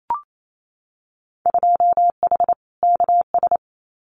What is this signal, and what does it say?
Morse code, heard as a keyed tone, about 28 wpm, 700 Hz. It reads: 2H KS
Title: Winter Field Day Exchanges Encore 28wpm